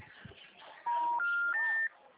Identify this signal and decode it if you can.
Special information tone.
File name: out- -2010- - - .31713